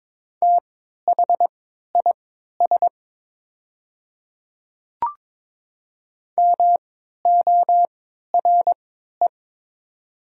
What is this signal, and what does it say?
Morse code, heard as a keyed tone, about 22 wpm, spaced as for 13 wpm, 700 Hz. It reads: THIS MORE
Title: 3x Character Spacing - Top 100 Words - 22wpm